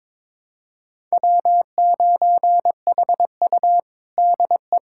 Morse code, heard as a keyed tone, about 21 wpm, 700 Hz. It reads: W9HU DE
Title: USA Callsign DE Callsign - 4 Characters - 22wpm